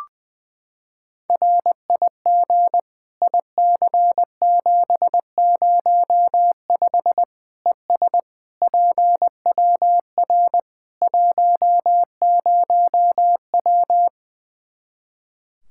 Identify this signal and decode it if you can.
Morse code, heard as a keyed tone, about 20 wpm, 700 Hz. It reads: RIG IC705 ES PWR 10W
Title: Practice QSO - Exchange 2 - Info - 20wpm